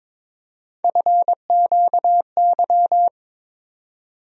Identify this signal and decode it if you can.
Morse code, heard as a keyed tone, about 22 wpm, 700 Hz. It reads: FQY
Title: USA Call Sign Postfixes - 3 Letters 22wpm